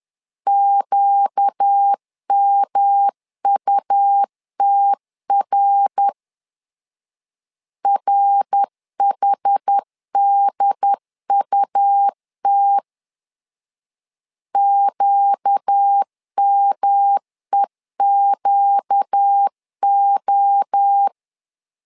Morse code, heard as a keyed tone, about 10 wpm, 800 Hz. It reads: QMUTR RHDUT QMEQO
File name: LEZIONI / 4 Q 10wpm 2210